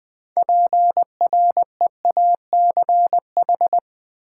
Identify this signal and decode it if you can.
Morse code, heard as a keyed tone, about 20 wpm, 700 Hz. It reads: PREACH
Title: Build A Word - 3 Letters At A Time - 6-Letter Words - 20wpm